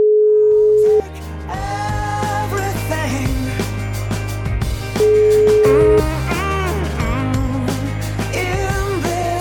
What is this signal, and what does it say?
Ringback tone (Europe, 425 Hz), with noise.